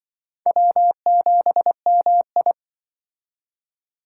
Morse code, heard as a keyed tone, about 24 wpm, 700 Hz. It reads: W7MI
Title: Speed Racing - USA 1x2 Call Signs 20wpm